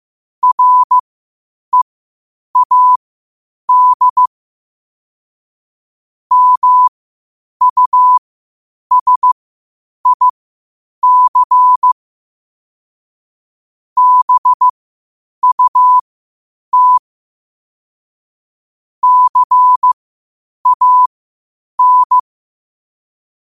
Morse code, heard as a keyed tone, about 15 wpm, spaced as for 8 wpm, 1000 Hz. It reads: READ MUSIC BUT CAN
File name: qotdfiles / 8wpm